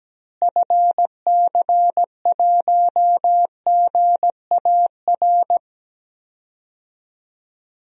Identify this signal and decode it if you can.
Morse code, heard as a keyed tone, about 17 wpm, 700 Hz. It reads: FC1GAR
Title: DXCC-100 Call Signs - 6 Characters - Repeated 3x - 17wpm